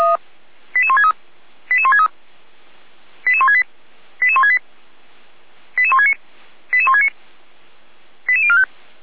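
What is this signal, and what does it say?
Five-tone selective call (ZVEI), heard as quick runs of five tones, tones of about 70 ms. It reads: 88163 88163 88168 88168 88169 88169 88045
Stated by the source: ZVEI five-tone call received by SDR.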